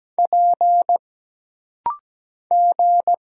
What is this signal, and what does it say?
Morse code, heard as a keyed tone, about 17 wpm, 700 Hz. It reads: P G
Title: Single Letters - Rapid-Fire 17wpm